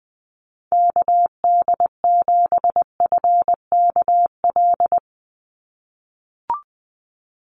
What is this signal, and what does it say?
Morse code, heard as a keyed tone, about 20 wpm, 700 Hz. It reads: KD7FKL
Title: USA Call Signs 20wpm